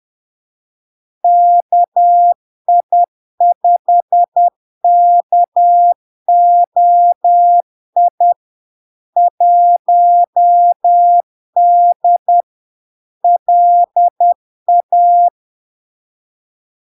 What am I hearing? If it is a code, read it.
Morse code, heard as a keyed tone, about 10 wpm, 700 Hz. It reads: KI5KOI 1D LA
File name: LARK-FD-2022-Log-in-CW-10wpm-700-Hz